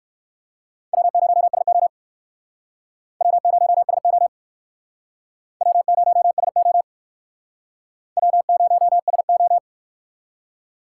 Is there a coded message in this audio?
Yes: Morse code from 0.9 to 9.6 s, heard as a keyed tone, about 60 wpm, 700 Hz. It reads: W0SO W0SO W0SO W0SO